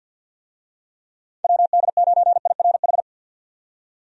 Morse code, heard as a keyed tone, about 51 wpm, 700 Hz. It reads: WD9IRH
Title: USA Call Signs - 6 Characters - 50wpm